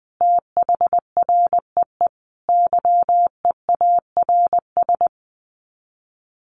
Morse code, heard as a keyed tone, about 19 wpm, 700 Hz. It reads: THREE YEARS
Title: Sets of 5 Words Encore2 20wpm